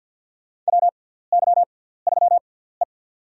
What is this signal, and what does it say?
Morse code, heard as a keyed tone, about 53 wpm, spaced as for 17 wpm, 690 Hz. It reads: WY3E
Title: USA Call Signs - 4 Characters - 6x Character Spacing - 50wpm